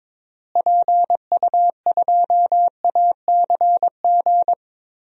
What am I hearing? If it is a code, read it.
Morse code, heard as a keyed tone, about 21 wpm, 700 Hz. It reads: PU2ACG
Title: DXCC-100 Call Signs - 6 Characters - Repeated 2x - 22wpm